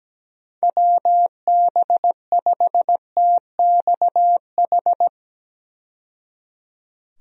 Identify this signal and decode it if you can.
Morse code, heard as a keyed tone, about 17 wpm, 700 Hz. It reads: WB5TXH
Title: USA Call Signs - 6 Characters - 17wpm